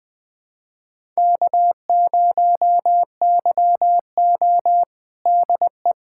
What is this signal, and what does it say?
Morse code, heard as a keyed tone, about 21 wpm, 700 Hz. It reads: K0YO DE
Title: USA Callsign DE Callsign - 4 Characters - 20wpm